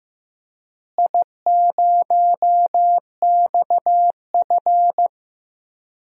Morse code, heard as a keyed tone, about 15 wpm, 700 Hz. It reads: I0XF
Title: DXCC-100 Call Signs - 4 Characters - Repeated 2x - 15wpm